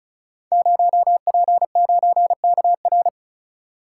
Morse code, heard as a keyed tone, about 36 wpm, 700 Hz. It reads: 0P9KR
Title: Code Groups - 5 Letter:Numbers - 35wpm